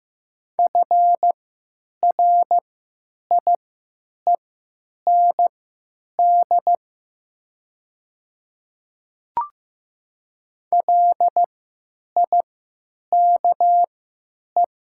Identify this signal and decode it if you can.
Morse code, heard as a keyed tone, about 15 wpm, spaced as for 9 wpm, 700 Hz. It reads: FRIEND LIKE
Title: 3x Character Spacing - Top 300 Words - 15wpm